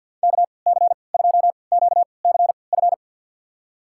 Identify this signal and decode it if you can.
Morse code, heard as a keyed tone, about 51 wpm, spaced as for 28 wpm, 690 Hz. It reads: KC2QCF
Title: USA Call Signs - 6 Characters - 3x Character Spacing - 50wpm